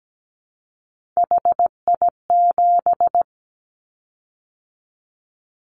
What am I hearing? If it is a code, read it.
Morse code, heard as a keyed tone, about 17 wpm, 700 Hz. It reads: HI7
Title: Callsign Prefixes for Top 100 DXCC - 17wpm